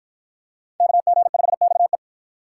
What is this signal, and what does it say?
Morse code, heard as a keyed tone, about 52 wpm, 700 Hz. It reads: KG5XE